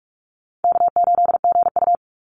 Morse code, heard as a keyed tone, about 44 wpm, 700 Hz. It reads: K8GU
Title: Easy Call Signs - 45wpm